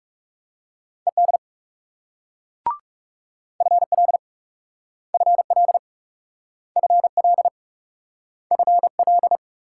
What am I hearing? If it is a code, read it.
Morse code, heard as a keyed tone, about 38 wpm, 700 Hz. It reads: ED FL FL FL FL